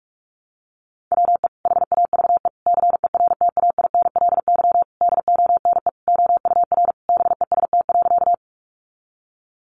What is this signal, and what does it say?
Morse code, heard as a keyed tone, about 44 wpm, 700 Hz. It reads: WE HAVE CERTAINLY DONE OUR BEST.
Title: Sentences from Top 900 Words 45wpm